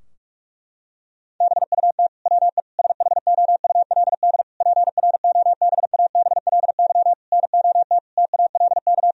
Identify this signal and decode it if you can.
Morse code, heard as a keyed tone, about 46 wpm, 700 Hz. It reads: BUT WE SHOULD PROBABLY NOT TALK
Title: Sentences from Top 1000 Words - Review Part 4 - 45wpm